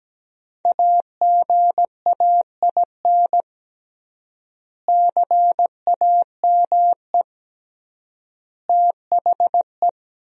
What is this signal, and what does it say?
Morse code, heard as a keyed tone, about 17 wpm, 700 Hz. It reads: AGAIN CAME THE
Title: 3x Word-Spacing - Sentences from Top 500 Words - 17wpm